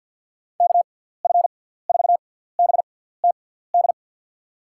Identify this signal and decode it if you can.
Morse code, heard as a keyed tone, about 48 wpm, spaced as for 17 wpm, 700 Hz. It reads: KF4BTD
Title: USA Call Signs - 6 Characters - 6x Character Spacing - 50wpm